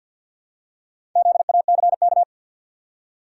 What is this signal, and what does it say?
Morse code, heard as a keyed tone, about 53 wpm, 700 Hz. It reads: ZACK
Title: USA Common Names - 50wpm